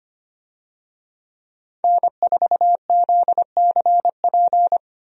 Morse code, heard as a keyed tone, about 25 wpm, 700 Hz. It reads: N4ZCP
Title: USA Call Signs - 5 Characters - Repeated 2x - 25wpm